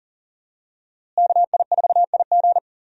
Morse code, heard as a keyed tone, about 37 wpm, 700 Hz. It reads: KI4IG
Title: USA Call Signs - 5 Characters - Repeated 3x - 40wpm